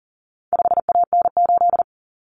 Morse code, heard as a keyed tone, about 40 wpm, 700 Hz. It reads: 5AN8
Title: Common Cut Numbers - 4-Digit 40wpm